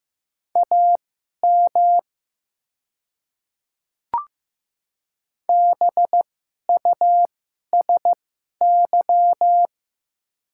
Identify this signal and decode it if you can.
Morse code, heard as a keyed tone, about 15 wpm, spaced as for 11 wpm, 700 Hz. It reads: AM BUSY
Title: 2x Character Spacing - Top 500 Words - 15wpm